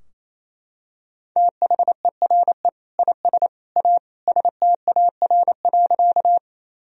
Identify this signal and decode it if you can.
Morse code, heard as a keyed tone, about 28 wpm, 700 Hz. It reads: THERE IS A STAR.
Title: Sentences from Top 1000 Words - Review Part 2 - 28wpm